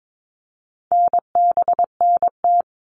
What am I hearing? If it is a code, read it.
Morse code, heard as a keyed tone, about 22 wpm, 700 Hz. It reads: NBNT
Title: All Cut Numbers - 4-Digit 22wpm